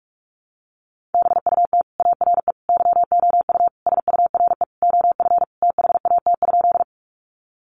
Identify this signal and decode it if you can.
Morse code, heard as a keyed tone, about 44 wpm, 700 Hz. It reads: BUT ARE YOU SURE OF THAT?
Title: Sentences from Top 300 Words 45wpm